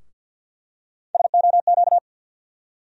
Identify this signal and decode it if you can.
Morse code, heard as a keyed tone, about 51 wpm, 700 Hz. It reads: SOQ
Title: USA Call Sign Postfixes - 3 Letters 50wpm